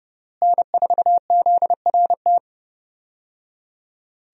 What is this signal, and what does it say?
Morse code, heard as a keyed tone, about 30 wpm, 700 Hz. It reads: N4ZRT
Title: USA Call Signs - 5 Characters - Repeated 3x - 30wpm